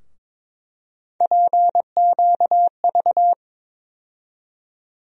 Morse code, heard as a keyed tone, about 22 wpm, 700 Hz. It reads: PQV
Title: USA Call Sign Postfixes - 3 Letters 22wpm